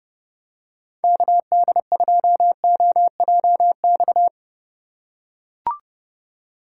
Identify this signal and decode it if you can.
Morse code, heard as a keyed tone, about 30 wpm, 700 Hz. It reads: KD2OJX
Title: USA Call Signs - 6 Characters - 2x Character Spacing - 30wpm